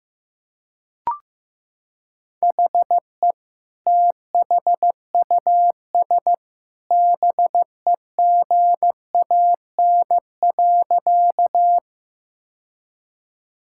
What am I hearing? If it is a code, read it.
Morse code, heard as a keyed tone, about 15 wpm, 700 Hz. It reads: HE THUS BEGAN.
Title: Sentences from Top 1000 Words - Review Part 5 - 15wpm